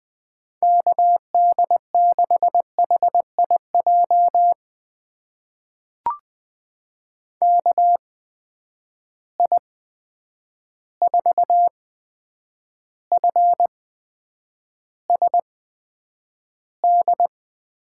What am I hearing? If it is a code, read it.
Morse code, heard as a keyed tone, about 20 wpm, spaced as for 9 wpm, 700 Hz. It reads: KD6HIJ KI4FSD
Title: USA Call Signs - 8x Character Spacing - 20wpm